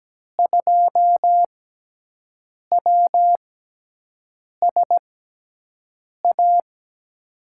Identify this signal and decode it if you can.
Morse code, heard as a keyed tone, about 17 wpm, spaced as for 6 wpm, 700 Hz. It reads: 2WSA
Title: USA Call Signs - 6 Characters - 6x Character Spacing - 17wpm